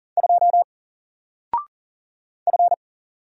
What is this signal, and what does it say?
Morse code, heard as a keyed tone, about 40 wpm, 700 Hz. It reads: 2 F
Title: Single Letter-Number 40wpm